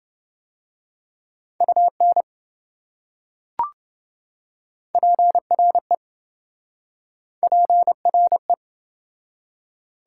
Morse code, heard as a keyed tone, about 27 wpm, 700 Hz. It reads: UN PRE PRE